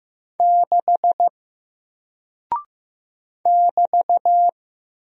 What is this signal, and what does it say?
Morse code, heard as a keyed tone, about 15 wpm, 700 Hz. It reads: 6 <BT>
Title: Single Character (Letter:Number:Punctuation:Prosign) Alt 15wpm